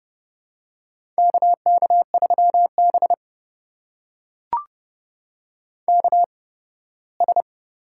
Morse code, heard as a keyed tone, about 30 wpm, spaced as for 18 wpm, 700 Hz. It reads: KK3B KS
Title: USA Call Signs - 4 Characters - 8x Character Spacing - 30wpm